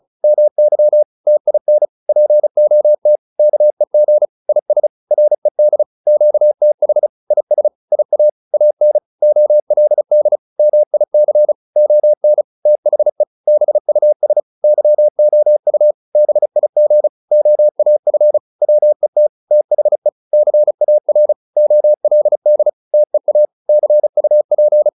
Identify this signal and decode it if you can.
Morse code, heard as a keyed tone, about 34 wpm, 600 Hz. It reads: MY TIN POT KEG IS RED QTH IS IA AN OLD MIC ON THE BUS YOU BIG OAF WET THE CAR OLD TEA CUP